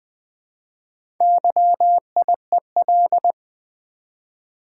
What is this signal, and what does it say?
Morse code, heard as a keyed tone, about 20 wpm, 700 Hz. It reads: YIEL
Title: Build A Word - 1 Letter At A Time - 5-Letter Words - 20wpm